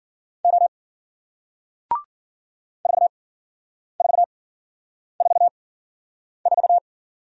Morse code, heard as a keyed tone, about 54 wpm, spaced as for 9 wpm, 700 Hz. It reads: O 4444